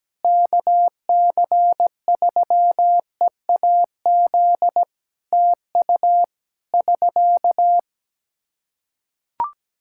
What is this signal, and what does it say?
Morse code, heard as a keyed tone, about 17 wpm, 700 Hz. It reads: KC3EAZ TU <SK>
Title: Practice QSO - Exchange 4 - Ending - 17wpm